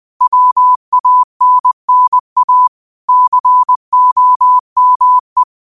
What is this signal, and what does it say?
Morse code, heard as a keyed tone, about 20 wpm, 1000 Hz. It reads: WANNA COME
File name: qotdfiles / 20wpm